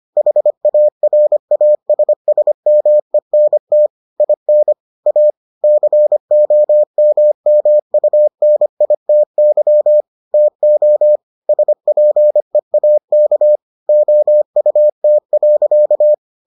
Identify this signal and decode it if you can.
Morse code, heard as a keyed tone, about 25 wpm, 600 Hz. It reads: HARASSMENT IN A COMMUNITY TO SPEAK OUT.